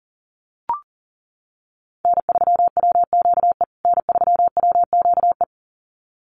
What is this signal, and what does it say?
Morse code, heard as a keyed tone, about 37 wpm, 700 Hz. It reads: N3WQE N3WQE
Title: USA Call Signs Repeated 2x 40wpm